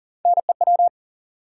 Morse code, heard as a keyed tone, about 40 wpm, 700 Hz. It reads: NEW
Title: Sentences from Top 300 Words - New Word Drill - 40wpm